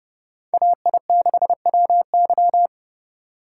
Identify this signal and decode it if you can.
Morse code, heard as a keyed tone, about 29 wpm, 700 Hz. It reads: AI6WY